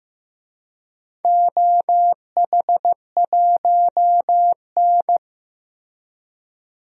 Morse code, heard as a keyed tone, about 15 wpm, 700 Hz. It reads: OH1N
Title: DXCC-100 Call Signs - 4 Characters - Repeated 3x - 15wpm